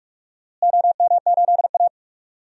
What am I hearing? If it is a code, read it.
Morse code, heard as a keyed tone, about 46 wpm, 700 Hz. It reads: OM8A
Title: DXCC-100 Call Signs - 4 Characters - Repeated 2x - 45wpm